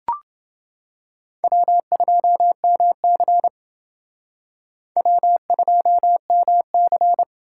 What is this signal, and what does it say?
Morse code, heard as a keyed tone, about 29 wpm, 700 Hz. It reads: W2MC W2MC